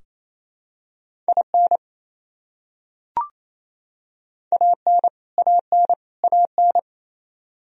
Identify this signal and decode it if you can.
Morse code, heard as a keyed tone, about 28 wpm, 700 Hz. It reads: IN AN AN AN
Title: Top 100 Words Repeated 28wpm